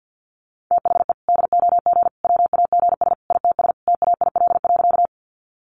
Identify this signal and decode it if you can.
Morse code, heard as a keyed tone, about 50 wpm, 700 Hz. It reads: THE DOG WAGS ITS TAIL.
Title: Easy Phrases 50wpm